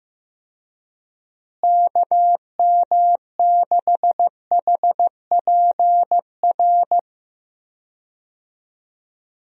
Morse code, heard as a keyed tone, about 15 wpm, 700 Hz. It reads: KM6HPR
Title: USA Call Signs - 6 Characters - Repeated 3x - 15wpm